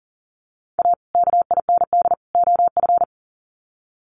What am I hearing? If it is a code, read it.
Morse code, heard as a keyed tone, about 40 wpm, 700 Hz. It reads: A KIND OF
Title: Sets of 3 Words 40wpm